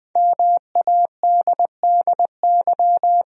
Morse code, heard as a keyed tone, about 20 wpm, 700 Hz. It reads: MADDY
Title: USA Common Names - Max 5 Letters - 20wpm